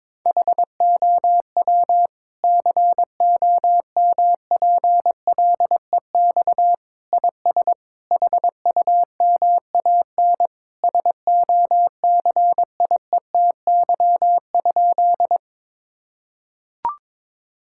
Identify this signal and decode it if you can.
Morse code, heard as a keyed tone, about 22 wpm, 700 Hz. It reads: HOW COMPLEX IS HUMAN SOCIETY?
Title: Sentences from Top 1000 Words - Review Part 6 - 22wpm